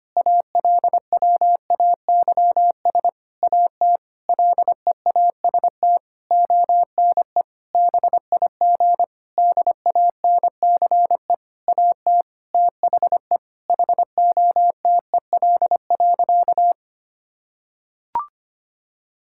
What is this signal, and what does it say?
Morse code, heard as a keyed tone, about 25 wpm, 700 Hz. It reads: ALWAYS AT LEAST ONE BIG DANCE AT THE HOTEL.
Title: Sentences from Top 1600 Words 25wpm